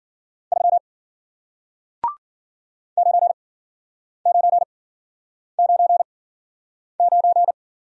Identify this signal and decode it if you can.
Morse code, heard as a keyed tone, about 53 wpm, spaced as for 9 wpm, 700 Hz. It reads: 3 9999